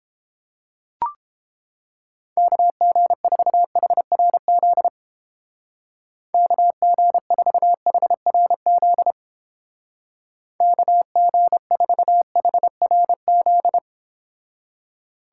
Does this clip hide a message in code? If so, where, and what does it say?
Yes: Morse code from 2.4 to 13.8 s, heard as a keyed tone, about 31 wpm, 700 Hz. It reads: KG4HRZ KG4HRZ KG4HRZ